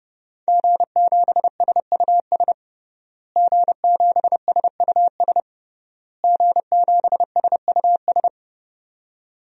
Morse code, heard as a keyed tone, about 30 wpm, 700 Hz. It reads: G7SUS G7SUS G7SUS
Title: DXCC-100 Call Signs - 5 Characters - Repeated 3x - 30wpm